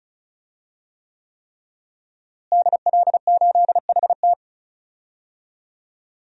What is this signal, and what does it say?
Morse code, heard as a keyed tone, about 34 wpm, 700 Hz. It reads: DL8HT
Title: DXCC-100 Call Signs - 5 Characters - Repeated 3x - 35wpm